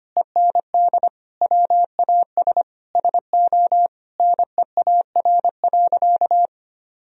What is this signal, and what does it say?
Morse code, heard as a keyed tone, about 25 wpm, 700 Hz. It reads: END WAS SO NEAR.
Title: Sentences from Top 1000 Words - Review Part 1 - 25wpm